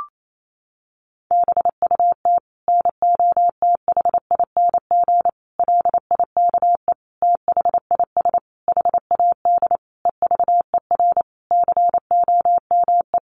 Morse code, heard as a keyed tone, about 28 wpm, 700 Hz. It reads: BUT NOTHING LIKE THIS HAD EVER COME
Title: Sentences from Top 1100 Words 28wpm